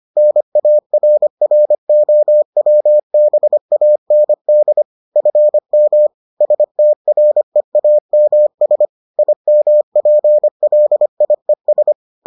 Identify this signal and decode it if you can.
Morse code, heard as a keyed tone, about 25 wpm, 600 Hz. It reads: NARROWBAND FM STREAMS IMPLIES